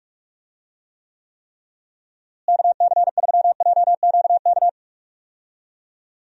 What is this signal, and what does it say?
Morse code, heard as a keyed tone, about 46 wpm, 700 Hz. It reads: KC3JQK
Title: USA Call Signs - 6 Characters - Repeated 3x - 45wpm